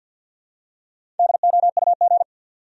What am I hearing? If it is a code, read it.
Morse code, heard as a keyed tone, about 53 wpm, 700 Hz. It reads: DOUG
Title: USA Common Names - 50wpm